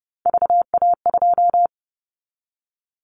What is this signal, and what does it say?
Morse code, heard as a keyed tone, about 30 wpm, 700 Hz. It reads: VA2
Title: Callsign Prefixes for Top 25 DXCC - Repeated 3x - 30wpm